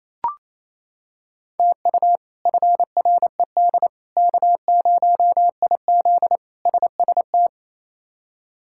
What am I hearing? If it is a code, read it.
Morse code, heard as a keyed tone, about 27 wpm, 700 Hz. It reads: TU FRED K0IZ SST
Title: K1USN SST QSO Practice - 28wpm